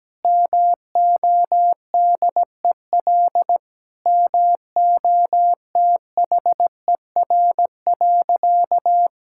Morse code, heard as a keyed tone, about 17 wpm, 700 Hz. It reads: MODEL MOTHER.
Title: Sentences from Top 1000 Words - Review Part 5 - 17wpm